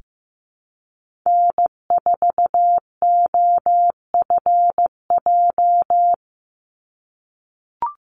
Morse code, heard as a keyed tone, about 15 wpm, 700 Hz. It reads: N4OFJ
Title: USA Call Signs 15wpm